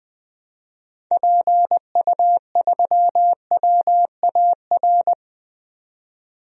Morse code, heard as a keyed tone, about 20 wpm, 700 Hz. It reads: PU3WAR
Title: DXCC-100 Call Signs - 6 Characters - 20wpm